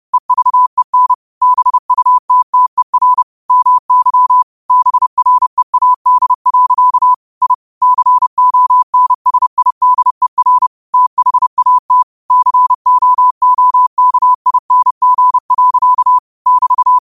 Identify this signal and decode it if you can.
Morse code, heard as a keyed tone, about 30 wpm, 1000 Hz. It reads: EVEN BUTTER MY BREAD. I CONSIDER THAT COOKING. <BT>